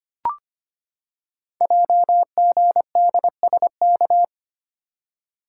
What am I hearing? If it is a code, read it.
Morse code, heard as a keyed tone, about 25 wpm, 700 Hz. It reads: JGDSK
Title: Code Groups - 5 Letter:Numbers - 25wpm